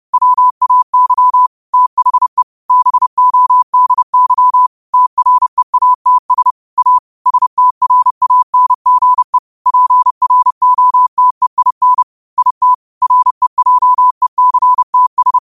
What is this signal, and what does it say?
Morse code, heard as a keyed tone, about 30 wpm, 1000 Hz. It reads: WAY THE BODY TREATS A STRANGE PROTEIN IT REJECTS
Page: 30 WPM Morse code quotes